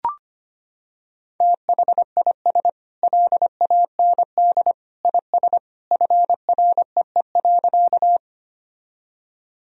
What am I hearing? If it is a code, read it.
Morse code, heard as a keyed tone, about 25 wpm, 700 Hz. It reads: THIS LAND IS FREE.
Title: Sentences from Top 1000 Words - Review Part 1 - 25wpm